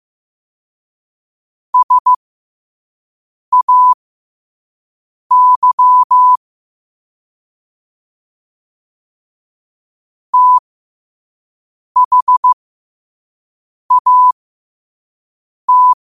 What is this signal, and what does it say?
Morse code, heard as a keyed tone, about 15 wpm, spaced as for 5 wpm, 1000 Hz. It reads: SAY THAT
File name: qotdfiles / 5wpm